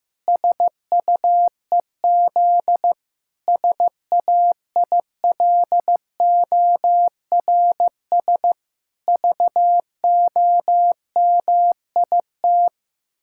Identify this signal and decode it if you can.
Morse code, heard as a keyed tone, about 16 wpm, 700 Hz. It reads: SUEZ SAILORS VOMIT
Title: Pangram Challenge v2 - 15wpm